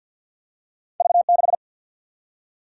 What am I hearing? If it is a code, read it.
Morse code, heard as a keyed tone, about 48 wpm, 700 Hz. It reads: V6
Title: Single-Letter Single-Number 50wpm